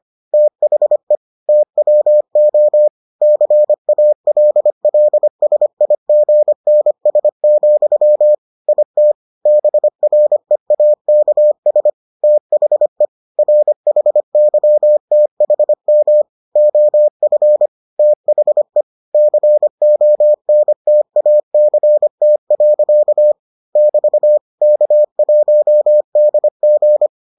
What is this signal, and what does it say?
Morse code, heard as a keyed tone, about 25 wpm, 600 Hz. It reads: THE TWO CALLSIGNS, IT BREAKS THE RHYTHM OF THE CONTACT. <BT> K1DG